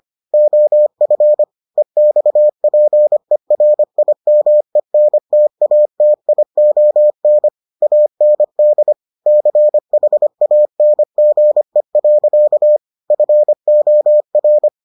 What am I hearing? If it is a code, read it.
Morse code, heard as a keyed tone, about 25 wpm, 600 Hz. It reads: OF EXPERIMENTATION AND CHANGE. FOR